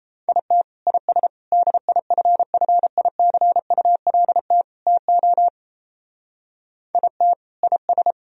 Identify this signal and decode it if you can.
Morse code, heard as a keyed tone, about 32 wpm, 700 Hz. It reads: IT IS DIFFICULT TO IT IS
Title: Speed-Racing - Sets of 4 Words 28wpm